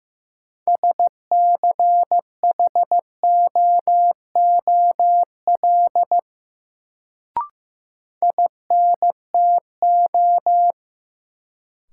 Morse code, heard as a keyed tone, about 15 wpm, 700 Hz. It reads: SCHOOL INTO
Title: Sets of 2 Words - New Word Drill - 15wpm